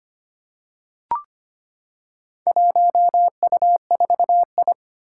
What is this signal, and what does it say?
Morse code, heard as a keyed tone, about 25 wpm, 700 Hz. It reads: 1U4I
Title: Four Challenging Letter-Number Combinations 25wpm